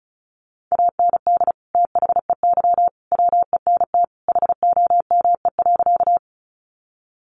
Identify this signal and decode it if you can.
Morse code, heard as a keyed tone, about 34 wpm, 700 Hz. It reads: AND THEY WENT HOME.
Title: Sentences from Top 200 Words 35wpm